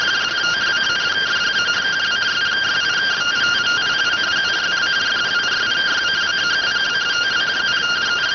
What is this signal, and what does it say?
Two-tone teleprinter signal, about 170 Hz shift.